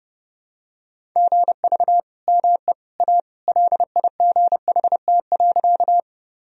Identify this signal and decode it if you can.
Morse code, heard as a keyed tone, about 30 wpm, 700 Hz. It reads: GV ME A LIGHT.
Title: Sentences from Top 200 Words and Abbreviations 30wpm